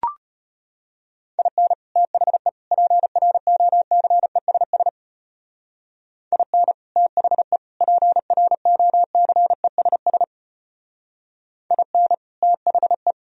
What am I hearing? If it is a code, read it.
Morse code, heard as a keyed tone, about 30 wpm, 700 Hz. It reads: IN THE PROCESS IN THE PROCESS IN THE
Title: Speed-Racing - Sets of 3 Words 25wpm